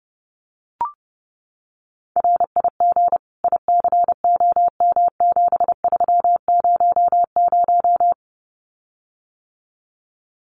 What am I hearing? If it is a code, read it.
Morse code, heard as a keyed tone, about 28 wpm, 700 Hz. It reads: RIG ICOM7300
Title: Common QSO Phrases Encore 30wpm